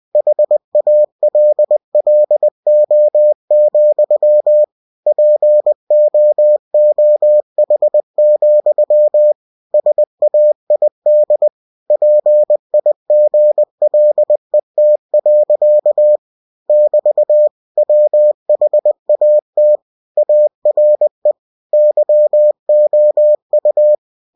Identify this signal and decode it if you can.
Morse code, heard as a keyed tone, about 20 wpm, 600 Hz. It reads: HALLO, POOH, SAID PIGLET. <BT> WHAT ARE YOU
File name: cw-books / winnie-the-pooh / winnie-20wpm-page-0048